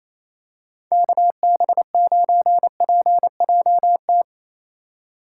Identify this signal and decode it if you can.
Morse code, heard as a keyed tone, about 28 wpm, 700 Hz. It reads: KB9PJT
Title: USA Call Signs - 6 Characters - 3x Character Spacing - 28wpm